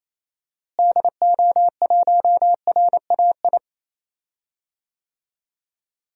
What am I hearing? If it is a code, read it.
Morse code, heard as a keyed tone, about 27 wpm, 700 Hz. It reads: DO1RAI
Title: DXCC-100 Call Signs - 6 Characters - Repeated 2x - 28wpm